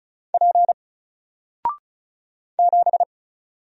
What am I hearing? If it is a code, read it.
Morse code, heard as a keyed tone, about 36 wpm, 700 Hz. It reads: P 7
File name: Single-Letter-Number-35wpm